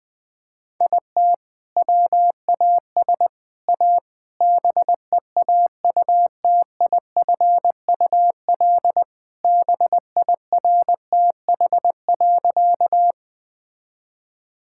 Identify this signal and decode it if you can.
Morse code, heard as a keyed tone, about 20 wpm, 700 Hz. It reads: IT WAS A BEAUTIFUL BIRTH.
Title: Sentences from Top 1000 Words - Review Part 5 - 20wpm